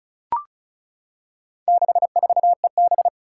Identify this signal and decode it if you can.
Morse code, heard as a keyed tone, about 36 wpm, 700 Hz. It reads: 64EB
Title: Four Challenging Letter-Number Combinations 35wpm